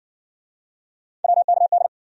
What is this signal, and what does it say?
Morse code, heard as a keyed tone, about 52 wpm, 700 Hz. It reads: WKD